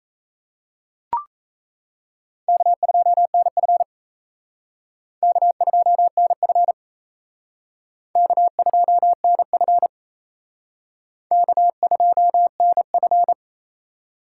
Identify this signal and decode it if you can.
Morse code, heard as a keyed tone, about 35 wpm, 700 Hz. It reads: K2NF K2NF K2NF K2NF